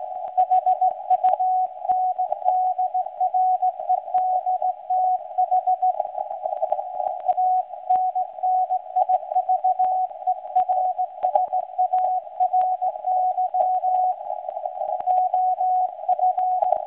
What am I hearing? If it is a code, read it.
Morse code, heard as a keyed tone, about 17 wpm, 740 Hz. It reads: HUNDREDTH ANNIVERSARY OF
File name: WebSDR records / SAQ recording start 17.2kHz